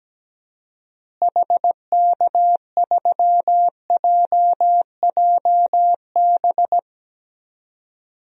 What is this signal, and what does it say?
Morse code, heard as a keyed tone, about 17 wpm, 700 Hz. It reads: HK3JJB
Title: DXCC-100 Call Signs - 6 Characters - Repeated 2x - 17wpm